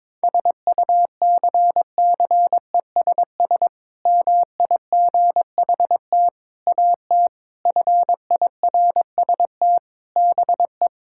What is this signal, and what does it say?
Morse code, heard as a keyed tone, about 22 wpm, 700 Hz. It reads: SUCCESS MIGHT AT FIRST BE
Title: Sentences from Top 1000 Words - Review Part 6 - 22wpm